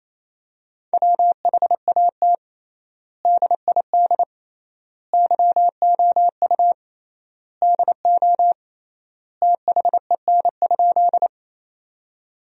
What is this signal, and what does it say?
Morse code, heard as a keyed tone, about 28 wpm, 700 Hz. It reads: WHAT DID YOU DO THEN?
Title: 3x Word-Spacing - Sentences from Top 100 Words - 28wpm